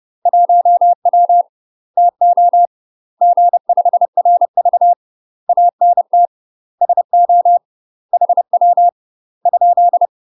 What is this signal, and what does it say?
Morse code, heard as a keyed tone, about 30 wpm, 700 Hz. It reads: 1W TO G5RV ANT SO HW ?
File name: UK MASTER QSOs TRAINING / 30WPM / UK MASTER QSOs 0024